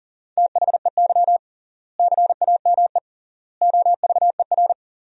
Morse code, heard as a keyed tone, about 40 wpm, 700 Hz. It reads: THEY CAME OVER
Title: 3x Word-Spacing - Sentences from Top 300 Words - 40wpm